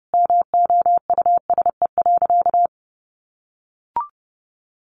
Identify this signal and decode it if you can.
Morse code, heard as a keyed tone, about 30 wpm, 700 Hz. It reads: MOUSE.
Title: A Sound Of Thunder 30wpm